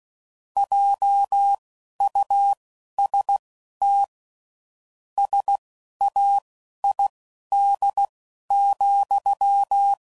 Morse code, heard as a keyed tone, about 16 wpm, spaced as for 11 wpm, 800 Hz. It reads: JUST SAID,
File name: CW Code Groups @10WPM/Technical Subjects